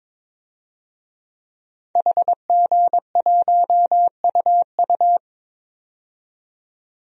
Morse code, heard as a keyed tone, about 22 wpm, 700 Hz. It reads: HG1UU
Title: DXCC-100 Call Signs - 5 Characters - 22wpm